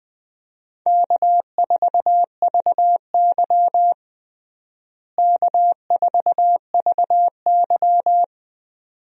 Morse code, heard as a keyed tone, about 20 wpm, 700 Hz. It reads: K4VY K4VY
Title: USA Call Signs - 4 Characters - Repeated 2x - 20wpm